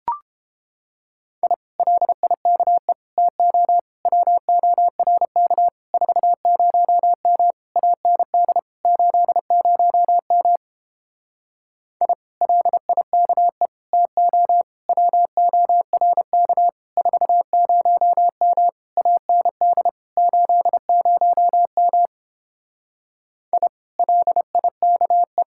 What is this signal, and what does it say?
Morse code, heard as a keyed tone, about 29 wpm, 700 Hz. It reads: I LIKE TO WORK 40M AND 80M I LIKE TO WORK 40M AND 80M I LIKE